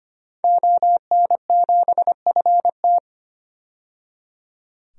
Morse code, heard as a keyed tone, about 25 wpm, 700 Hz. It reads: ON7FT
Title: DXCC-100 Call Signs - 5 Characters - 25wpm